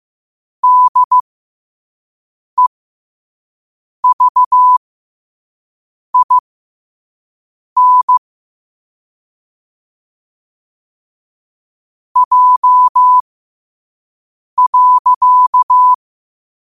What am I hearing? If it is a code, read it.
Morse code, heard as a keyed tone, about 15 wpm, spaced as for 5 wpm, 1000 Hz. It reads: DEVIN J.
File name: qotdfiles / 5wpm